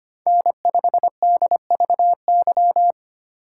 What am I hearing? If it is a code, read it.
Morse code, heard as a keyed tone, about 25 wpm, 700 Hz. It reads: N5DVY
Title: USA Call Signs - 3x Character Spacing - 25wpm